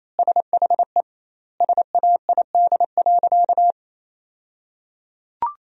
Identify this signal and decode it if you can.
Morse code, heard as a keyed tone, about 28 wpm, 700 Hz. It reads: SHE SAID.
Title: 2x Word-Spacing - Sentences from Top 400 Words - 28wpm